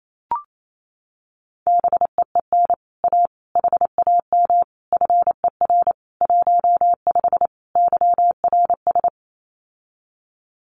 Morse code, heard as a keyed tone, about 28 wpm, 700 Hz. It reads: BEEN A HAM FER 15 YRS
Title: Common QSO Phrases Encore 28wpm